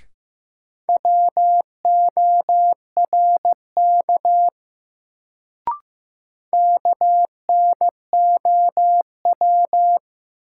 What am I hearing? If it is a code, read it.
Morse code, heard as a keyed tone, about 15 wpm, 700 Hz. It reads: WORK KNOW
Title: Sets of 2 Words - New Word Drill - 15wpm